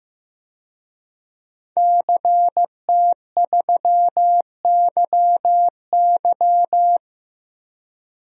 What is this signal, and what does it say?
Morse code, heard as a keyed tone, about 15 wpm, 700 Hz. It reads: CT3YY
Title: DXCC-100 Call Signs - 5 Characters - Repeated 3x - 15wpm